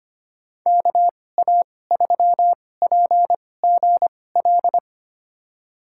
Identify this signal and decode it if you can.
Morse code, heard as a keyed tone, about 25 wpm, spaced as for 18 wpm, 700 Hz. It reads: KA3PGL
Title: USA Call Signs - 6 Characters - 2x Character Spacing - 25wpm